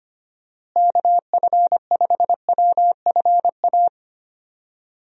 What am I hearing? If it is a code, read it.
Morse code, heard as a keyed tone, about 25 wpm, 700 Hz. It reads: KF5WFA
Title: USA Call Signs - 6 Characters - Repeated 2x - 25wpm